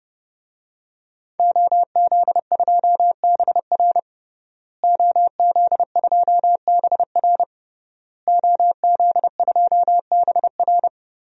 Morse code, heard as a keyed tone, about 30 wpm, 700 Hz. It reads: OZ2BR OZ2BR OZ2BR
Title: DXCC-100 Call Signs - 5 Characters - Repeated 3x - 30wpm